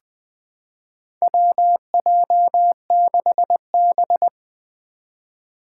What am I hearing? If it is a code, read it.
Morse code, heard as a keyed tone, about 20 wpm, 700 Hz. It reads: WJ6B
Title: Four Challenging Letter-Number Combinations 20wpm